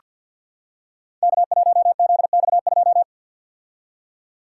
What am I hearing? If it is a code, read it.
Morse code, heard as a keyed tone, about 46 wpm, 690 Hz. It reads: K1ZX2
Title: Code Groups - 5 Letter:Numbers - 50wpm